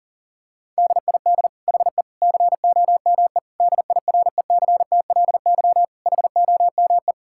Morse code, heard as a keyed tone, about 37 wpm, 700 Hz. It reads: DID HE COME DIRECTLY HOME
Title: Sentences from Top 1000 Words - Review Part 5 - 40wpm